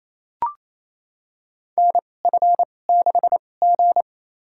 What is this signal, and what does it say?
Morse code, heard as a keyed tone, about 28 wpm, spaced as for 20 wpm, 700 Hz. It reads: NF6G
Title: USA Call Signs - 4 Characters - 2x Character Spacing - 28wpm